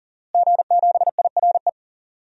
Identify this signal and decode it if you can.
Morse code, heard as a keyed tone, about 40 wpm, 700 Hz. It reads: G7IRE
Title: DXCC-100 Call Signs - 5 Characters - Repeated 2x - 40wpm